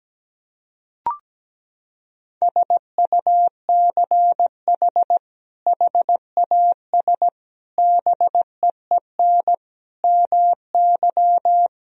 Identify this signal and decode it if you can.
Morse code, heard as a keyed tone, about 17 wpm, 700 Hz. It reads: SUCH HAS BEEN MY
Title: Sentences from Top 1000 Words - Review Part 3 - 17wpm